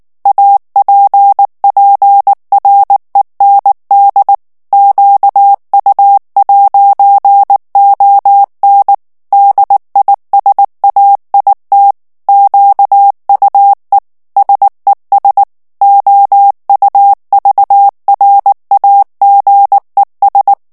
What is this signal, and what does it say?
Morse code, heard as a keyed tone, about 19 wpm, 800 Hz. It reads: APPREND QU'ON DISAIT QUE SES OUVRAGES
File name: FRANCAIS TEXTE CLAIR 2 VITESSE 18 MOTS MINUTES